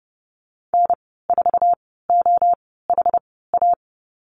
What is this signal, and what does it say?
Morse code, heard as a keyed tone, about 30 wpm, spaced as for 17 wpm, 700 Hz. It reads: N4OHA
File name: Lesson 14g - Number 2 - Call Signs 30wpm